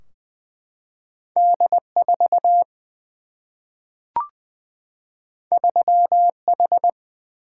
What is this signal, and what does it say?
Morse code, heard as a keyed tone, about 20 wpm, 700 Hz. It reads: D 4 3 H
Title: Single-Letter Single-Number 20wpm